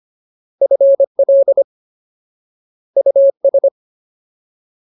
Morse code, heard as a keyed tone, about 25 wpm, 550 Hz. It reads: FL US
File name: Ultimate Word List / 25WPM Ultimate